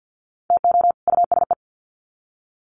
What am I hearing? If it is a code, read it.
Morse code, heard as a keyed tone, about 50 wpm, 700 Hz. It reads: TO USE
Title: Sets of 2 Words 50wpm